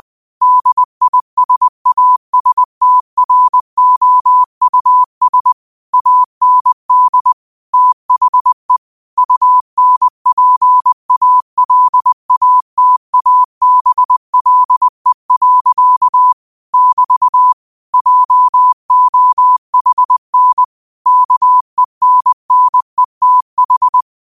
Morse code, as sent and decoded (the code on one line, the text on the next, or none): -.. .. ... .- ... - .-. --- ..- ... / .- -. -.. / - .... / ..- -. .--. .- .-.. .- - .- -... .-.. .-.-.- / -...- / .--- --- .... -. / -.- . -. -. - ....
DISASTROUS AND THE UNPALATABLE. <BT> JOHN KENNETH